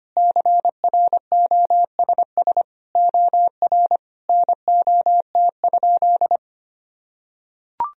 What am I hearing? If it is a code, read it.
Morse code, heard as a keyed tone, about 25 wpm, 700 Hz. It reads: CROSS OR NOT?
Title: Sentences from Top 1000 Words - Review Part 3 - 25wpm